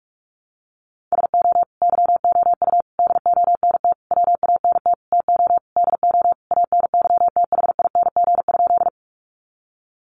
Morse code, heard as a keyed tone, about 44 wpm, 700 Hz. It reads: SO YOU DONT WANT TO DO ANYTHING?
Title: Sentences from Top 1000 Words 45wpm